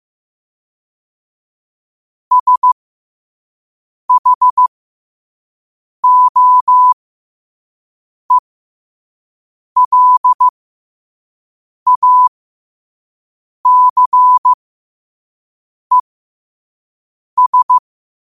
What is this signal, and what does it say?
Morse code, heard as a keyed tone, about 15 wpm, spaced as for 5 wpm, 1000 Hz. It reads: SHOELACES
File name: qotdfiles / 5wpm